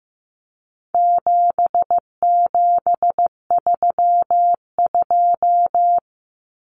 Morse code, heard as a keyed tone, about 15 wpm, 700 Hz. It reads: 7732
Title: Common Cut Numbers - 4-Digit 15wpm